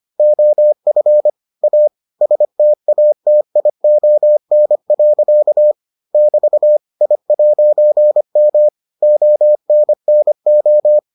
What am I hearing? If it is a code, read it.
Morse code, heard as a keyed tone, about 25 wpm, 600 Hz. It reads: OF A STATION. <BT> I'M ONNO